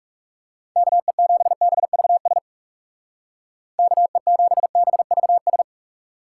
Morse code, heard as a keyed tone, about 35 wpm, 700 Hz. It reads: KE7BVS KE7BVS
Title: Speed Racing - USA Call Signs 30wpm